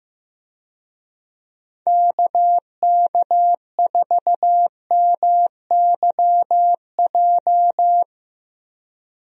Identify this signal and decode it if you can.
Morse code, heard as a keyed tone, about 15 wpm, 700 Hz. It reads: KK4MYJ
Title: USA Call Signs - 6 Characters - Repeated 2x - 15wpm